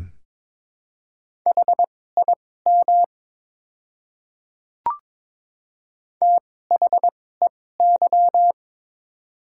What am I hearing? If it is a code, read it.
Morse code, heard as a keyed tone, about 23 wpm, spaced as for 16 wpm, 690 Hz. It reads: HIM THEY
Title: 2x Character Spacing - Top 500 Words - 22wpm